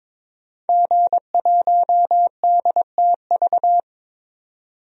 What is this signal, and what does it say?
Morse code, heard as a keyed tone, about 22 wpm, 700 Hz. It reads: G1DTV
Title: DXCC-100 Call Signs - 5 Characters - 22wpm